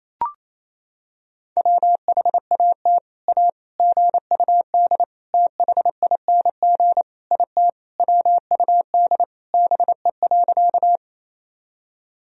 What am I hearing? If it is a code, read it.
Morse code, heard as a keyed tone, about 28 wpm, 700 Hz. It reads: WHAT A GUD THING IT WUD BE.
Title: Sentences from Top 200 Words and Abbreviations 28wpm